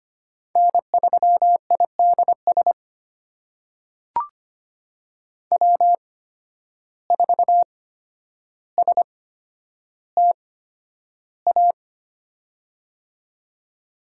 Morse code, heard as a keyed tone, about 25 wpm, spaced as for 11 wpm, 700 Hz. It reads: N3IDS W4STA
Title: USA Call Signs - 5 Characters - 8x Character Spacing - 25wpm